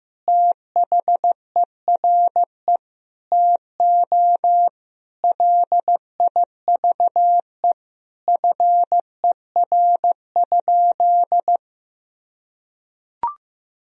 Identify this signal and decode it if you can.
Morse code, heard as a keyed tone, about 15 wpm, 700 Hz. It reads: THERE TO LIVE FER?
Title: Sentences from Top 200 Words and Abbreviations 15wpm